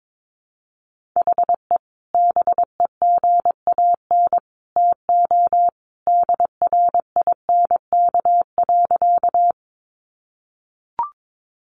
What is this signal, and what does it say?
Morse code, heard as a keyed tone, about 22 wpm, 700 Hz. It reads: HE BEGAN TO DRINK.
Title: Sentences from Top 1500 Words 22wpm